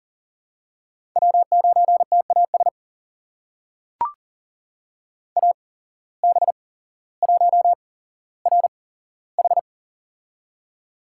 Morse code, heard as a keyed tone, about 40 wpm, spaced as for 17 wpm, 700 Hz. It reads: W9TAS AB1RH
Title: USA Call Signs - 5 Characters - 8x Character Spacing - 40wpm